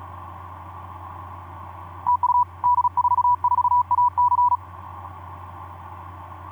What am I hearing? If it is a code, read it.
Morse code, heard as a keyed tone, about 37 wpm, 980 Hz. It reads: TU D44AC
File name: d44ac ek2rgt etc 10m cw 23nov2010 1202